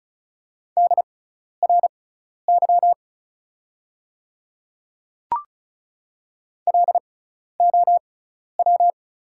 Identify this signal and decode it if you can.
Morse code, heard as a keyed tone, about 34 wpm, spaced as for 12 wpm, 690 Hz. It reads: DRY LOW
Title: 6x Character Spacing - Top 500 Words - 35wpm